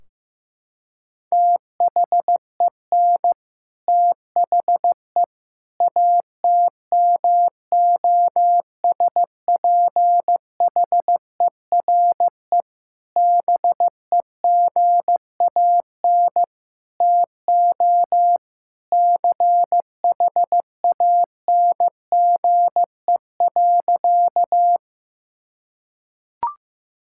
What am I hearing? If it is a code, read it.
Morse code, heard as a keyed tone, about 15 wpm, 700 Hz. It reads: THEN THE ATMOSPHERE BEGAN TO CHANGE.
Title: Sentences from Top 1600 Words 15wpm